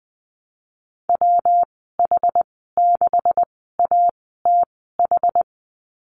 Morse code, heard as a keyed tone, about 20 wpm, spaced as for 14 wpm, 700 Hz. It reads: WH6ATH
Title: 245 Lesson 39g - Letter z - Call Signs 20wpm